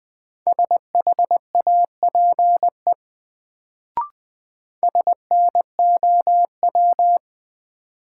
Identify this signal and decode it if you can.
Morse code, heard as a keyed tone, about 20 wpm, 700 Hz. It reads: SHAPE SNOW
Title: Sentences from Top 500 Words - New Word Drill - 20wpm